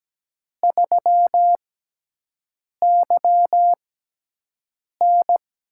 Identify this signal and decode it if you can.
Morse code, heard as a keyed tone, about 17 wpm, spaced as for 6 wpm, 700 Hz. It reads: 3YN
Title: USA Call Signs - 4 Characters - 6x Character Spacing - 17wpm